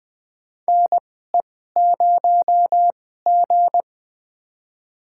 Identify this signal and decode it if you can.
Morse code, heard as a keyed tone, about 20 wpm, spaced as for 15 wpm, 700 Hz. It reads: NE0G
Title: USA Call Signs - 4 Characters - 2x Character Spacing - 20wpm